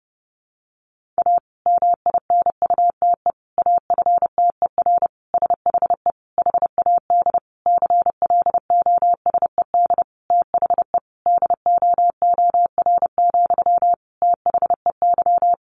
Morse code, heard as a keyed tone, about 30 wpm, 700 Hz. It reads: A MINUTE AFTER SHE HAD CLOSED THE DOOR, THEY
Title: Sentences from Top 1400 Words 30wpm